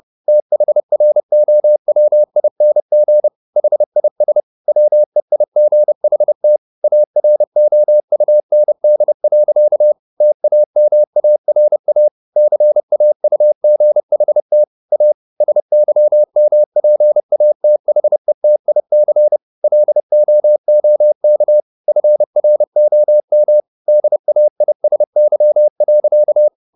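Morse code, heard as a keyed tone, about 30 wpm, 600 Hz. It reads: THROWING HIS WEIGHT AROUND. TAMARA CAUGHT A SYMPATHETIC LOOK FROM DAISY.
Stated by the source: Morse code at 30 wpm, the novel Never by Ken Follett in English, from CW med Gnister.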